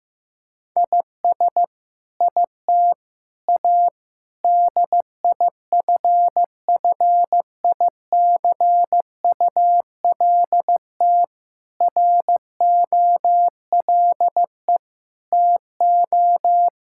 Morse code, heard as a keyed tone, about 15 wpm, 700 Hz. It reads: IS IT A DIFFICULT ROLE TO
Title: Sentences from Top 1000 Words - Review Part 6 - 15wpm